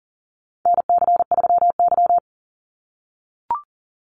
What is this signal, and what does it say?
Morse code, heard as a keyed tone, about 40 wpm, 700 Hz. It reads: NC3Y
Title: Call Signs - 40wpm